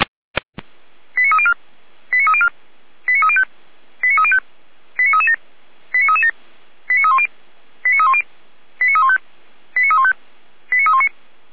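Five-tone selective call (ZVEI), heard as quick runs of five tones, tones of about 70 ms. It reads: 89304 89304 89305 89305 89337 89337 89310 89310 89315 89315 89319
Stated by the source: ZVEI five-tone selective call received by SDR, archived on the Feuerwehr Orlamünde fire brigade audio server.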